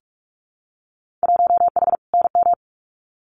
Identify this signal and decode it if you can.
Morse code, heard as a keyed tone, about 44 wpm, 700 Hz. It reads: 1H NM
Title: Winter Field Day Exchanges 45wpm